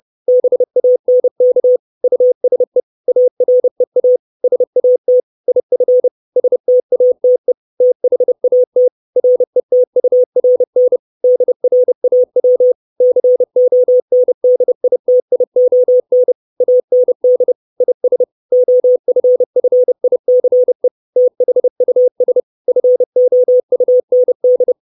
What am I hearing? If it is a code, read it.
Morse code, heard as a keyed tone, about 30 wpm, 500 Hz. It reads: BANK USE AREA SAT IF STATE THAT RETURN DRAW CONDITION AND IS OFFICE THUS FOUND
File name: 632 Words 30WPM